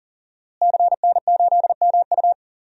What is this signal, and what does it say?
Morse code, heard as a keyed tone, about 37 wpm, 700 Hz. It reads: CN8MU
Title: DXCC-100 Call Signs - 5 Characters - Repeated 3x - 40wpm